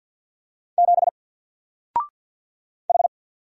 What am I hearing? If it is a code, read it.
Morse code, heard as a keyed tone, about 48 wpm, 700 Hz. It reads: 7 H
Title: Single Letter-Number Alt 50wpm